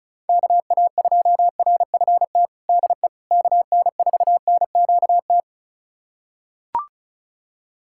Morse code, heard as a keyed tone, about 36 wpm, 700 Hz. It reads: KA2RFT DE KN4NQT
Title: USA Callsign DE Callsign 35wpm